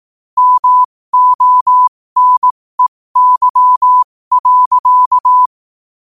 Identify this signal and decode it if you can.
Morse code, heard as a keyed tone, about 18 wpm, spaced as for 15 wpm, 1000 Hz. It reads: MONEY.
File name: qotdfiles / 15wpm